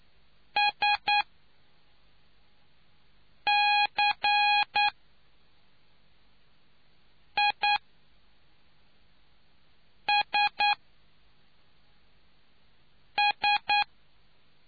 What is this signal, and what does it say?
Morse code, heard as a keyed tone, about 9 wpm, spaced as for 3 wpm, 830 Hz. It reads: SCISS